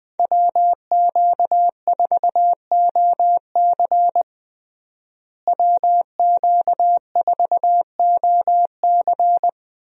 Morse code, heard as a keyed tone, about 20 wpm, 700 Hz. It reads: WQ4OC WQ4OC
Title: USA Call Signs - 5 Characters - Repeated 3x - 20wpm